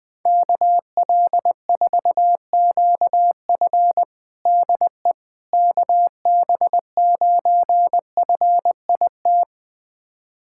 Morse code, heard as a keyed tone, about 20 wpm, 700 Hz. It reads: KL4QF DE KB9FIT
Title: USA Callsign DE Callsign 20wpm